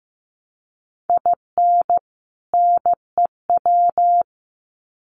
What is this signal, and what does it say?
Morse code, heard as a keyed tone, about 15 wpm, 700 Hz. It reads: IN NEW
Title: Sets of 2 Words 15wpm